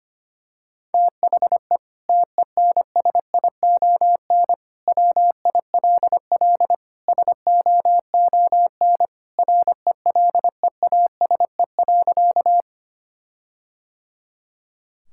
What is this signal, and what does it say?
Morse code, heard as a keyed tone, about 25 wpm, 700 Hz. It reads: THE TENSION WILL SOON RELEASE.
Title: Sentences from Top 1600 Words 25wpm